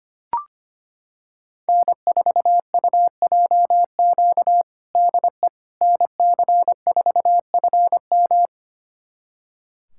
Morse code, heard as a keyed tone, about 25 wpm, 700 Hz. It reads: N4UJQ DE NC4FM
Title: USA Callsign DE Callsign - 5 Characters - 25wpm